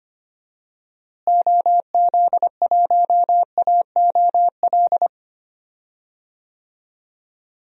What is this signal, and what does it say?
Morse code, heard as a keyed tone, about 25 wpm, 700 Hz. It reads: OZ1AOL
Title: DXCC-100 Call Signs - 6 Characters - Repeated 2x - 25wpm